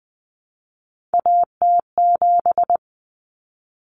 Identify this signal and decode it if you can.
Morse code, heard as a keyed tone, about 20 wpm, 700 Hz. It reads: AT7
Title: Common Cut Numbers - 3-Digit 20wpm